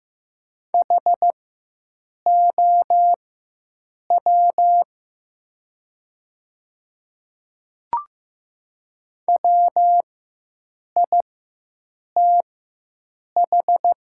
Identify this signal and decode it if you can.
Morse code, heard as a keyed tone, about 15 wpm, spaced as for 7 wpm, 700 Hz. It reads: HOW WITH
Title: 4x Character Spacing - Top 100 Words - 15wpm